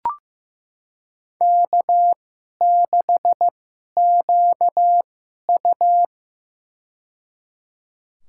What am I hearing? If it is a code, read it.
Morse code, heard as a keyed tone, about 16 wpm, spaced as for 11 wpm, 700 Hz. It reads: K6QU
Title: USA Call Signs - 2x Character Spacing - 15wpm